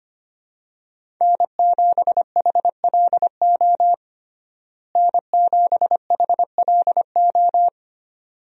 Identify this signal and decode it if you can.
Morse code, heard as a keyed tone, about 25 wpm, 700 Hz. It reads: N7HLO N7HLO
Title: USA Call Signs - 5 Characters - Repeated 2x - 25wpm